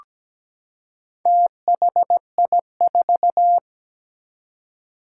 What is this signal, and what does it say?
Morse code, heard as a keyed tone, about 17 wpm, 700 Hz. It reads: THI4
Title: Four Challenging Letter-Number Combinations 17wpm